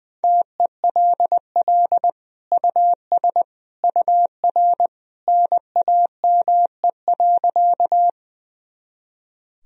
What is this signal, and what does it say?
Morse code, heard as a keyed tone, about 20 wpm, 700 Hz. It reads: TELL US UR NAME.
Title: Sentences from Top 200 Words and Abbreviations 20wpm